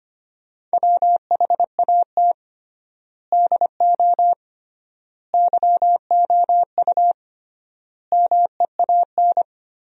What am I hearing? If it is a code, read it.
Morse code, heard as a keyed tone, about 25 wpm, 700 Hz. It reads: WHAT DO YOU MEAN
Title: 3x Word-Spacing - Sentences from Top 200 Words - 25wpm